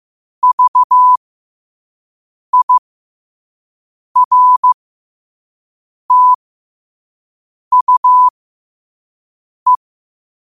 Morse code, heard as a keyed tone, about 15 wpm, spaced as for 5 wpm, 1000 Hz. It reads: VIRTUE
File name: qotdfiles / 5wpm